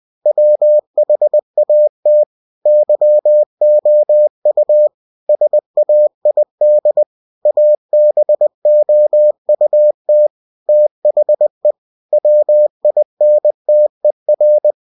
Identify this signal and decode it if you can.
Morse code, heard as a keyed tone, about 20 wpm, 600 Hz. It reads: WHAT YOU SAID ABOUT THE WINTER
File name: cw-books / winnie-the-pooh / winnie-20wpm-page-0065